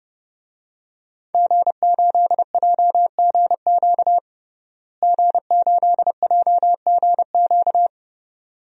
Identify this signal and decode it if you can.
Morse code, heard as a keyed tone, about 32 wpm, 700 Hz. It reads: G8JGQ G8JGQ
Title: DXCC-100 Call Signs - 5 Characters - Repeated 2x - 30wpm